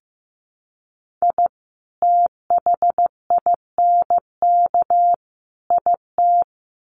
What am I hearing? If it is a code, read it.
Morse code, heard as a keyed tone, about 15 wpm, 700 Hz. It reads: I THINK IT
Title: Sets of 3 Words 15wpm